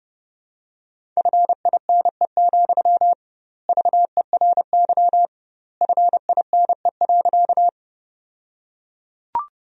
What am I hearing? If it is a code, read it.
Morse code, heard as a keyed tone, about 30 wpm, 700 Hz. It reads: FINE, VERY FINE.
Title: 2x Word-Spacing - Sentences from Top 500 Words - 30wpm